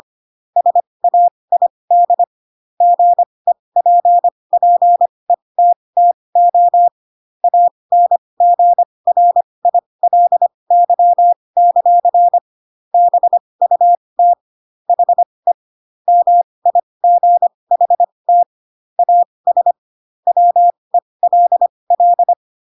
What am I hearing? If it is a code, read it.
Morse code, heard as a keyed tone, about 25 wpm, spaced as for 20 wpm, 700 Hz. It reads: SAID GEPPETTO ANGRILY; BUT HE MIGHT AS WELL
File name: Pinocchio / Pinocchio ch3 15 25 ik1hge